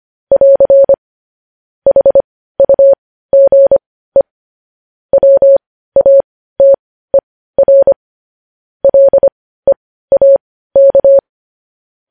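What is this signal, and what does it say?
Morse code, heard as a keyed tone, about 25 wpm, spaced as for 15 wpm, 550 Hz. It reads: <AR> HUGE WATER LEAK
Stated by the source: Morse news headlines practice at 15.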